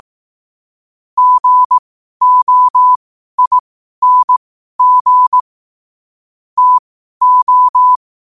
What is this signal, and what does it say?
Morse code, heard as a keyed tone, about 18 wpm, spaced as for 12 wpm, 1000 Hz. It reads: GOING TO
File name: qotdfiles / 12wpm